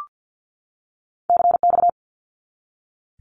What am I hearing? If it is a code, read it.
Morse code, heard as a keyed tone, about 50 wpm, 700 Hz. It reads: CX
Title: Abbreviations 50wpm